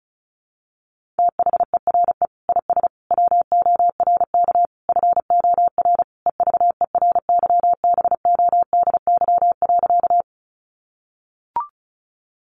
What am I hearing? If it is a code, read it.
Morse code, heard as a keyed tone, about 34 wpm, 700 Hz. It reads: THERE IS WORK FOR EVERYBODY.
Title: Sentences from Top 1400 Words 35wpm